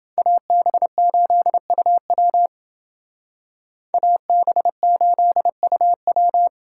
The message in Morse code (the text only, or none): AB8UW AB8UW